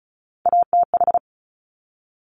Morse code, heard as a keyed tone, about 36 wpm, 700 Hz. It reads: ATH
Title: Top Three Letter Combinations in Words 35wpm